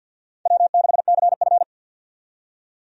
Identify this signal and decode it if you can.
Morse code, heard as a keyed tone, about 50 wpm, 700 Hz. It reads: W6CF
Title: USA Call Signs - 4 Characters - 50wpm